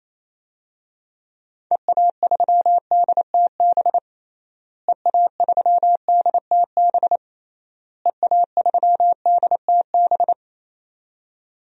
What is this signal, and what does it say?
Morse code, heard as a keyed tone, about 28 wpm, 700 Hz. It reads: EA3DTB EA3DTB EA3DTB
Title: DXCC-100 Call Signs - 6 Characters - Repeated 3x - 28wpm